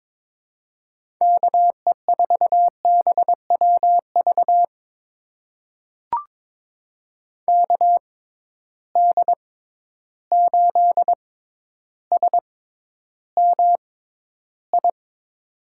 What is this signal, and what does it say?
Morse code, heard as a keyed tone, about 21 wpm, spaced as for 11 wpm, 700 Hz. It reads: KE4BWV KD8SMI
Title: USA Call Signs - 6 Characters - 6x Character Spacing - 22wpm